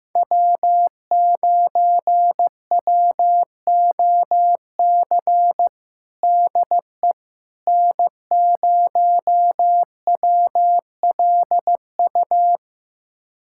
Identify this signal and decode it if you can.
Morse code, heard as a keyed tone, about 15 wpm, 700 Hz. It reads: W9WOC DE N0WLU
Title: USA Callsign DE Callsign - 5 Characters - 15wpm